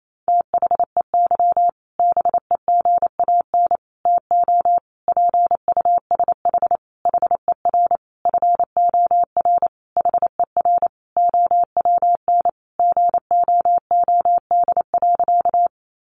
Morse code, heard as a keyed tone, about 28 wpm, 700 Hz. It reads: THEY BEGAN TO PUSH HER FOR HER OWN GOOD.
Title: Sentences from Top 1300 Words 28wpm